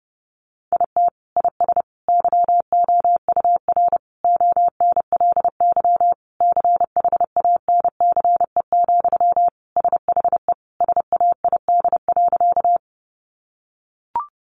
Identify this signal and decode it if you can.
Morse code, heard as a keyed tone, about 30 wpm, 700 Hz. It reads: IT IS YOUR ONLY CHANCE, SHE SAID.
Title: Sentences from Top 800 Words 30wpm